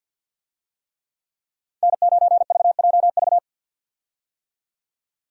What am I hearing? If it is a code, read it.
Morse code, heard as a keyed tone, about 48 wpm, 700 Hz. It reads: N9VJV
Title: USA Call Signs - 5 Characters - 50wpm